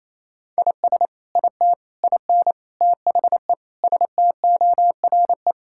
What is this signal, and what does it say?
Morse code, heard as a keyed tone, about 28 wpm, 700 Hz. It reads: IS IT IN THE STORE
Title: Sentences from Top 1000 Words - Review Part 5 - 28wpm